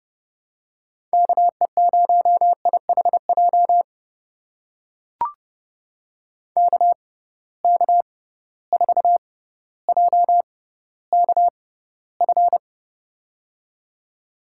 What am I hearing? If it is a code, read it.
Morse code, heard as a keyed tone, about 30 wpm, spaced as for 15 wpm, 700 Hz. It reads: KE0IHJ KK4JKF
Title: USA Call Signs - 6 Characters - 6x Character Spacing - 30wpm